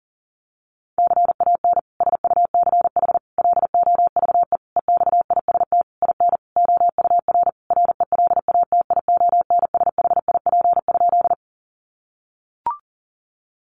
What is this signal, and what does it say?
Morse code, heard as a keyed tone, about 40 wpm, 700 Hz. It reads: CAN SUCH LOVE EXIST IN OUR RELATIONSHIP?
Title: Sentences from Top 1100 Words 40wpm